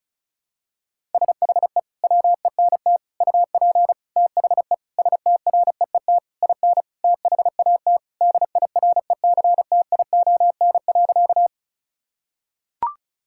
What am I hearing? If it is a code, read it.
Morse code, heard as a keyed tone, about 34 wpm, 700 Hz. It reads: SHE WENT UP THE STREET IN THAT DIRECTION.
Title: Sentences from Top 1000 Words - Review Part 5 - 35wpm